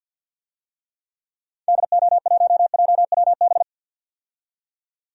Morse code, heard as a keyed tone, about 47 wpm, 700 Hz. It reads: DO1JWB
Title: DXCC-100 Call Signs - 6 Characters - 50wpm